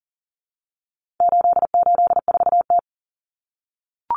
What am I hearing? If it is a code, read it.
Morse code, heard as a keyed tone, about 40 wpm, 700 Hz. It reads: 884T
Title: Common Cut Numbers - 4-Digit 40wpm